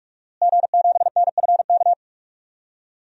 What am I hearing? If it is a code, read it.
Morse code, heard as a keyed tone, about 42 wpm, 700 Hz. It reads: G7NFK
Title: DXCC-100 Call Signs - 5 Characters - Repeated 3x - 45wpm